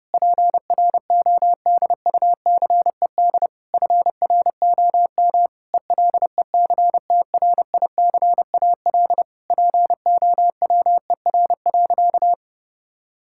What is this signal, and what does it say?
Morse code, heard as a keyed tone, about 30 wpm, 700 Hz. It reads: PRODUCED FROM ELECTRICAL POWER.
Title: Sentences from Top 1000 Words - Review Part 6 - 30wpm